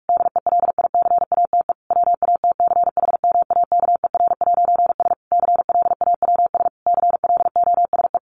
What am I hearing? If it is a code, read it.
Morse code, heard as a keyed tone, about 44 wpm, 700 Hz. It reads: DELICATE WATCHMAKER'S CLAWS CLOSE
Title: A Sound Of Thunder 45wpm